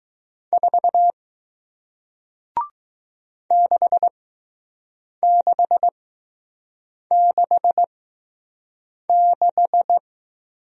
Morse code, heard as a keyed tone, about 20 wpm, spaced as for 6 wpm, 700 Hz. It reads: 4 6666